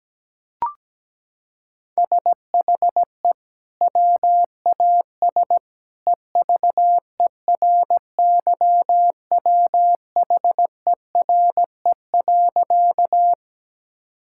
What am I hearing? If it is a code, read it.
Morse code, heard as a keyed tone, about 17 wpm, 700 Hz. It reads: SHE WAS EVERYWHERE.
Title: Sentences from Top 1600 Words 17wpm